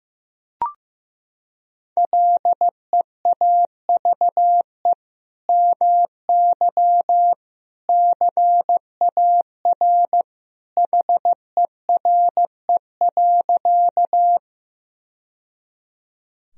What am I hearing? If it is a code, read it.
Morse code, heard as a keyed tone, about 15 wpm, 700 Hz. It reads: LEAVE MY CAR HERE.
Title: Sentences from Top 1000 Words - Review Part 1 - 15wpm